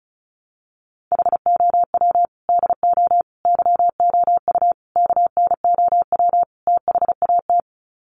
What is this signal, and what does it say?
Morse code, heard as a keyed tone, about 34 wpm, 700 Hz. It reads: HOW DO YOU KNOW THAT
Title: Sets of 5 Words Encore 35wpm